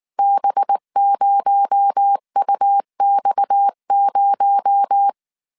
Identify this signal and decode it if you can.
Morse code, heard as a keyed tone, about 19 wpm, 800 Hz. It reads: B0UX0
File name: CW / DET / 1 6 1500C S A 18wpm 2255